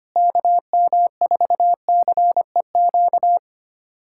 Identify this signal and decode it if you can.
Morse code, heard as a keyed tone, about 25 wpm, 700 Hz. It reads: KM4CEQ
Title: USA Callsign DE Callsign - 6 Characters - 25wpm